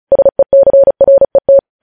Morse code, heard as a keyed tone, about 36 wpm, 550 Hz. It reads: SECRET